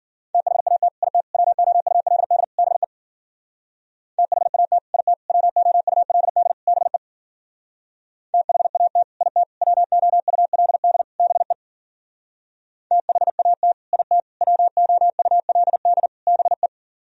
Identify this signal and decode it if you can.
Morse code, heard as a keyed tone, about 50 wpm, 700 Hz. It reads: THAT IT WOULD BE THAT IT WOULD BE THAT IT WOULD BE THAT IT WOULD BE
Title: Speed-Racing - Sets of 4 Words 40wpm